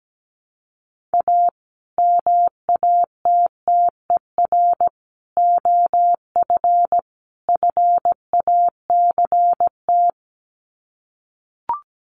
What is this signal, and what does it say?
Morse code, heard as a keyed tone, about 17 wpm, 700 Hz. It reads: A MATTER OF FACT
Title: Sets of 4 Words 17wpm